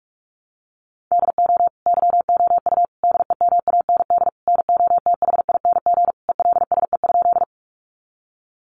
Morse code, heard as a keyed tone, about 44 wpm, 700 Hz. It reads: DO YOU DEMAND NOTHING ELSE?
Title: Sentences from Top 800 Words 45wpm